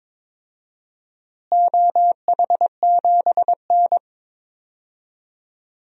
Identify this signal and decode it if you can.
Morse code, heard as a keyed tone, about 22 wpm, 700 Hz. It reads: OH7N
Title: DXCC-100 Call Signs - 4 Characters - Repeated 3x - 22wpm